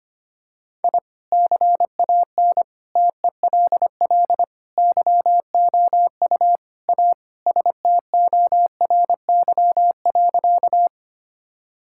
Morse code, heard as a keyed tone, about 25 wpm, 700 Hz. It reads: I CAN TELL YOU A STORY.
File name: practice / 25wpm / Sentences with 4-Word Phrases - 25wpm